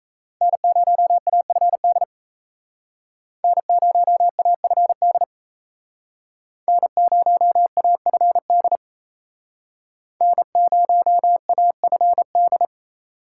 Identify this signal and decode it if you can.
Morse code, heard as a keyed tone, about 36 wpm, 700 Hz. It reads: N0AFD N0AFD N0AFD N0AFD